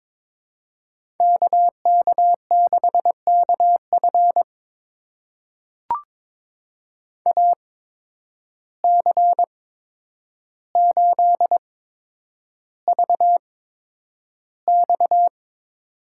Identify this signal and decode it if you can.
Morse code, heard as a keyed tone, about 22 wpm, spaced as for 9 wpm, 700 Hz. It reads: KK6KF AC8VX
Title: USA Call Signs - 5 Characters - 8x Character Spacing - 22wpm